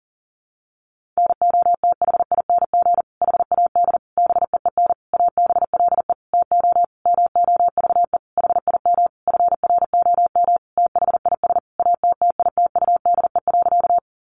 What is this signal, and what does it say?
Morse code, heard as a keyed tone, about 40 wpm, 700 Hz. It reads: NOTHING HAD BEEN ABLE TO MOVE HIM FROM THIS ATTITUDE.
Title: Sentences from Top 1300 Words 40wpm